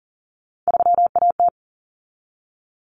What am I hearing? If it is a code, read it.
Morse code, heard as a keyed tone, about 40 wpm, 700 Hz. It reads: 3AT
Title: Common Cut Numbers - 3-Digit 40wpm